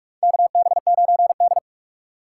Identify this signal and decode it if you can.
Morse code, heard as a keyed tone, about 43 wpm, 700 Hz. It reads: KB9D
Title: USA Call Signs - 4x Character Spacing - 45wpm